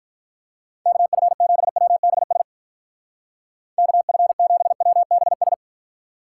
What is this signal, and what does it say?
Morse code, heard as a keyed tone, about 45 wpm, 700 Hz. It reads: KF7WBS KF7WBS